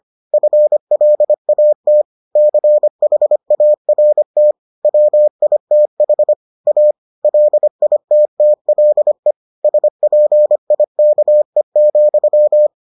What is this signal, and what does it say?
Morse code, heard as a keyed tone, about 25 wpm, 600 Hz. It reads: FLAT CHART WITH A LITTLE SPIKE,